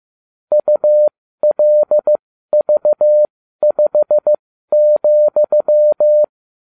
Morse code, heard as a keyed tone, about 15 wpm, spaced as for 12 wpm, 600 Hz. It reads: ULV5,